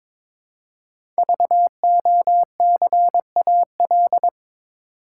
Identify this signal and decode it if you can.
Morse code, heard as a keyed tone, about 22 wpm, 700 Hz. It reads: VOCAL
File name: 5 Letter Words - First 3 Letters Given - 22wpm